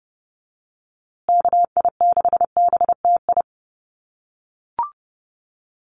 Morse code, heard as a keyed tone, about 30 wpm, 700 Hz. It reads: KI6BTI
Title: USA Call Signs Repeated 2x 30wpm